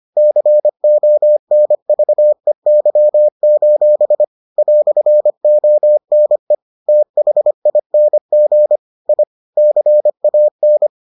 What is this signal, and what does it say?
Morse code, heard as a keyed tone, about 25 wpm, 600 Hz. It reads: CONVEY: "ONE THING I CAN